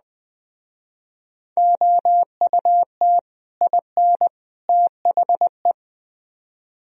Morse code, heard as a keyed tone, about 20 wpm, 700 Hz. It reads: OUT IN THE
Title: Speed-Racing - Sets of 3 Words 17wpm